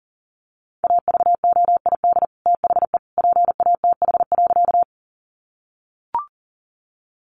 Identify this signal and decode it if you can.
Morse code, heard as a keyed tone, about 40 wpm, 700 Hz. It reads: AVOID THE PATH.
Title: Sentences from Top 1400 Words 40wpm